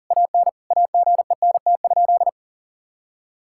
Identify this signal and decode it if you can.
Morse code, heard as a keyed tone, about 40 wpm, 700 Hz. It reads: AN AGENT?
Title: Sentences from Top 1600 Words 40wpm